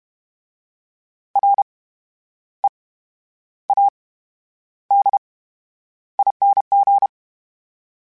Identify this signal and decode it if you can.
Morse code, heard as a keyed tone, about 32 wpm, spaced as for 11 wpm, 800 Hz. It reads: READING
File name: Special INGs / Special INGs-mp3s / VerySlowToFaster Ing @30